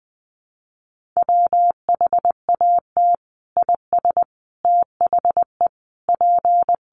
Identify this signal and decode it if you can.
Morse code, heard as a keyed tone, about 19 wpm, 700 Hz. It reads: WHAT IS THE P
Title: Sentences from Top 1400 Words 20wpm